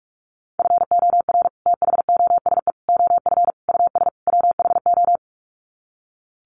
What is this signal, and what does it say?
Morse code, heard as a keyed tone, about 44 wpm, 700 Hz. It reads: FOR THOSE OF US WHO
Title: Sets of 5 Words Encore 45wpm